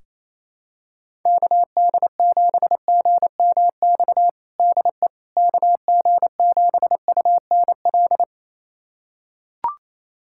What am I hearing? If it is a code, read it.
Morse code, heard as a keyed tone, about 28 wpm, 700 Hz. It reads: KD7GMX DE KG7UNL
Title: USA Callsign DE Callsign - 6 Characters - 28wpm